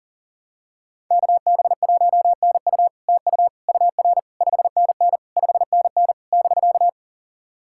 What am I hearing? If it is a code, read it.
Morse code, heard as a keyed tone, about 42 wpm, 700 Hz. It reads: KB1NU TU UR 5NN 5NN <BK>
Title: SOTA QSO Practice 40wpm